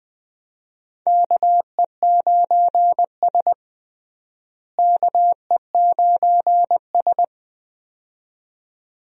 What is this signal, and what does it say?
Morse code, heard as a keyed tone, about 20 wpm, 700 Hz. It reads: KE9S KE9S
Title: USA Call Signs - 4 Characters - Repeated 2x - 20wpm